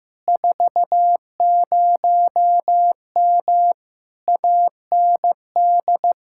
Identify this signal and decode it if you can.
Morse code, heard as a keyed tone, about 15 wpm, 700 Hz. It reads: 40M AND
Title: Speed-Racing Common QSO Phrases Encore - 15wpm